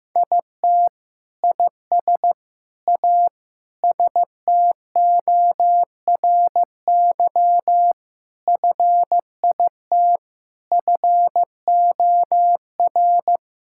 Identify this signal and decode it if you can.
Morse code, heard as a keyed tone, about 15 wpm, 700 Hz. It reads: IT IS A STORY FIT FOR
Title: Sentences from Top 1000 Words - Review Part 5 - 15wpm